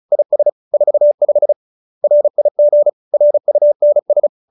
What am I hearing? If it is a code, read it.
Morse code, heard as a keyed tone, about 36 wpm, 590 Hz. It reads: IS 45 RIG RUNS